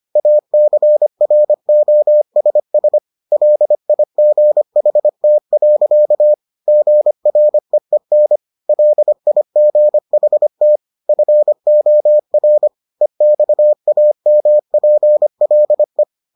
Morse code, heard as a keyed tone, about 25 wpm, 600 Hz. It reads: ACROSS LIGHT. GREEN LIGHT FOR EXAMPLE